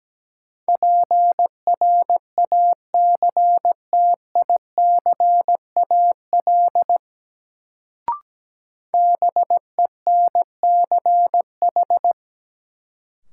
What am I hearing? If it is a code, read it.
Morse code, heard as a keyed tone, about 17 wpm, 700 Hz. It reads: PRACTICAL BENCH
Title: Sentences from Top 1400 Words - New Word Drill - 17wpm